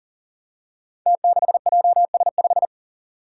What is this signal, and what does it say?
Morse code, heard as a keyed tone, about 39 wpm, 700 Hz. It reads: T6JS5
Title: Five Challenging Letter-Number Combinations 40wpm